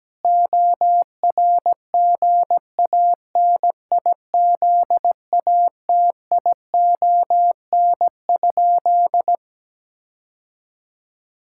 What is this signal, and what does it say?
Morse code, heard as a keyed tone, about 17 wpm, 700 Hz. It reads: ORGANIZATION?
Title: Sentences from Top 1000 Words - Review Part 5 - 17wpm